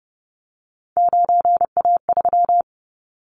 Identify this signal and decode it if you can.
Morse code, heard as a keyed tone, about 30 wpm, 700 Hz. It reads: 9A3
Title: Callsign Prefixes for Top 50 DXCC - 30wpm